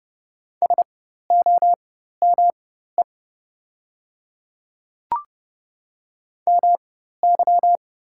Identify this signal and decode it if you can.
Morse code, heard as a keyed tone, about 30 wpm, spaced as for 14 wpm, 700 Hz. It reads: SOME MY
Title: 4x Character Spacing - Top 200 Words - 30wpm